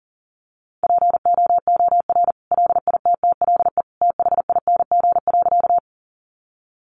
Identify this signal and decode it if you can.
Morse code, heard as a keyed tone, about 40 wpm, 700 Hz. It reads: POOR LITTLE THING.
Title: Sentences from Top 1000 Words 40wpm